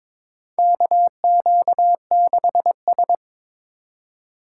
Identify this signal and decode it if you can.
Morse code, heard as a keyed tone, about 21 wpm, 700 Hz. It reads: KQ6S
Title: USA Call Signs - 4 Characters - 2x Character Spacing - 22wpm